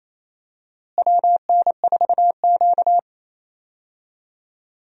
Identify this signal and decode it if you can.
Morse code, heard as a keyed tone, about 28 wpm, 700 Hz. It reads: WN4Q
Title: USA Call Signs - 4 Characters - Repeated 3x - 28wpm